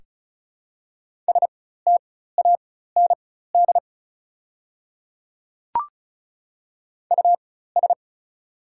Morse code, heard as a keyed tone, about 36 wpm, spaced as for 16 wpm, 700 Hz. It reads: STAND US
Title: 4x Character Spacing - Top 500 Words - 35wpm